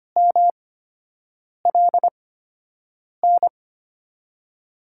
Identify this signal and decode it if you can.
Morse code, heard as a keyed tone, about 25 wpm, 700 Hz. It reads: M L N
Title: USA Call Signs - 8x Character Spacing - 25wpm